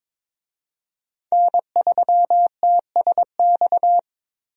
Morse code, heard as a keyed tone, about 22 wpm, 700 Hz. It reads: N3TSX
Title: USA Call Signs - 5 Characters - Repeated 2x - 22wpm